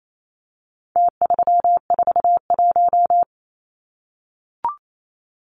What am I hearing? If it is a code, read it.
Morse code, heard as a keyed tone, about 28 wpm, 700 Hz. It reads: T341
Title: Numbers - Contest Style - 4-Digit 28wpm